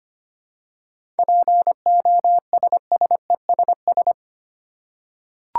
Morse code, heard as a keyed tone, about 25 wpm, 700 Hz. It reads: POSSESS
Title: Seven Letter Words - 25wpm